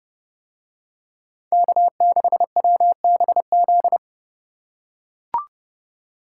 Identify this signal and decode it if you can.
Morse code, heard as a keyed tone, about 29 wpm, 700 Hz. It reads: K6WBZ